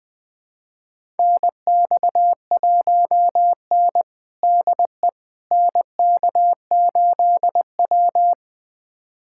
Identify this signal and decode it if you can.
Morse code, heard as a keyed tone, about 21 wpm, 700 Hz. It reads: NX1N DE NK8W
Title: USA Callsign DE Callsign - 4 Characters - 20wpm